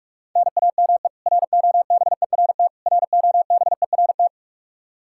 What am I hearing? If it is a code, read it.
Morse code, heard as a keyed tone, about 44 wpm, 700 Hz. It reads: NAME ROBERT ROBERT
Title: Practice QSO - Exchange 1 - Beginning - 45wpm